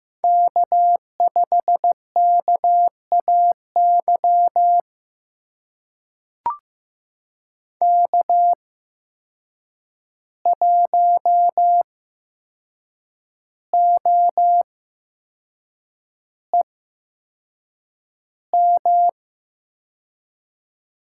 Morse code, heard as a keyed tone, about 15 wpm, spaced as for 6 wpm, 700 Hz. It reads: K5KAY K1OEM
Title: USA Call Signs - 5 Characters - 8x Character Spacing - 15wpm